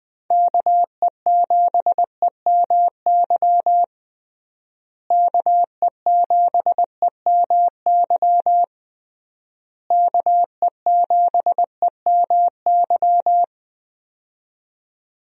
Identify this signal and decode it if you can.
Morse code, heard as a keyed tone, about 20 wpm, 700 Hz. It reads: KE7EMY KE7EMY KE7EMY
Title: USA Call Signs - 6 Characters - Repeated 3x - 20wpm